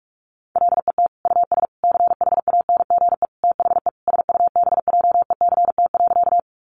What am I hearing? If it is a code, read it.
Morse code, heard as a keyed tone, about 44 wpm, 700 Hz. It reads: LET US CHANGE THE SUBJECT.
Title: Sentences from Top 800 Words 45wpm